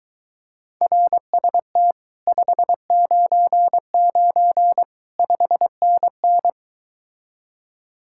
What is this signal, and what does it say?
Morse code, heard as a keyed tone, about 23 wpm, 700 Hz. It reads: RST 599 5NN